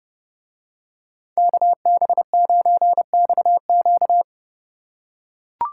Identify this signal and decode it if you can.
Morse code, heard as a keyed tone, about 29 wpm, 700 Hz. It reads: KB9XQ